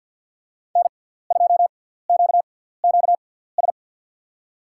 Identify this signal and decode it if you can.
Morse code, heard as a keyed tone, about 52 wpm, spaced as for 17 wpm, 700 Hz. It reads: N2QQS
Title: USA Call Signs - 5 Characters - 6x Character Spacing - 50wpm